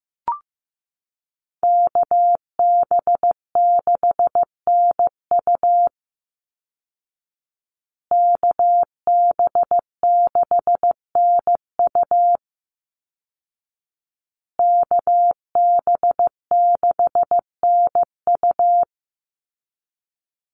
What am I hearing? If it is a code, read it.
Morse code, heard as a keyed tone, about 15 wpm, 700 Hz. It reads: KB6NU KB6NU KB6NU
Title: Call Signs Repeated 2x - 15wpm